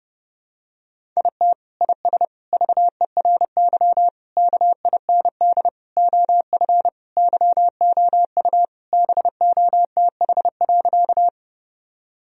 Morse code, heard as a keyed tone, about 30 wpm, 700 Hz. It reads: IT IS VERY KIND OF YOU BOTH.
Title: Sentences from Top 1000 Words - Review Part 1 - 30wpm